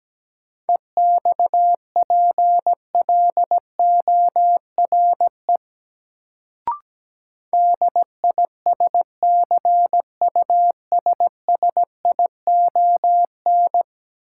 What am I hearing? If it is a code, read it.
Morse code, heard as a keyed tone, about 17 wpm, 700 Hz. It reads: EXPLORE DISCUSSION
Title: Sentences from Top 1100 Words - New Word Drill Alt - 17wpm